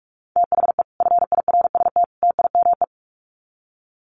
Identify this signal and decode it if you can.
Morse code, heard as a keyed tone, about 44 wpm, 700 Hz. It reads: THE FIRST TIME
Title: Sets of 3 Words 45wpm